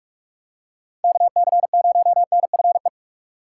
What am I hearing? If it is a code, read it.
Morse code, heard as a keyed tone, about 46 wpm, 700 Hz. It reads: KC0NFE
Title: USA Call Signs - 6x Character Spacing - 45wpm